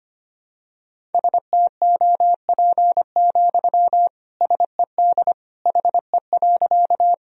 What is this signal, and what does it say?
Morse code, heard as a keyed tone, about 25 wpm, 700 Hz. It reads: STOP, SED HE.
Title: Sentences from Top 300 Words and Abbreviations 25wpm